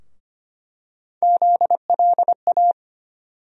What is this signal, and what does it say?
Morse code, heard as a keyed tone, about 25 wpm, 700 Hz. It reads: ZLA
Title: USA Call Sign Postfixes - 3 Letters 25wpm